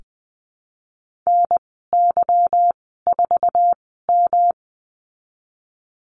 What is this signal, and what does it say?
Morse code, heard as a keyed tone, about 20 wpm, spaced as for 14 wpm, 700 Hz. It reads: NY4M
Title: 193 Lesson 31g - Letter v - Call Signs 20wpm